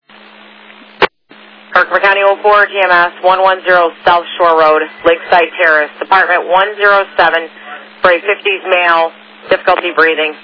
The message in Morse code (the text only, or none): none